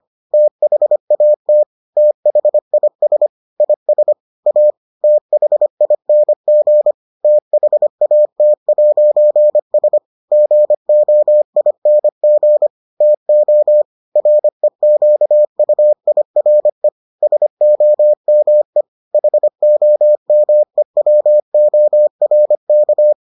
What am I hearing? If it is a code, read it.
Morse code, heard as a keyed tone, about 25 wpm, 600 Hz. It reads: THAT THIS IS A THING THAT'S GOING TO REQUIRE SOME HOMEWORK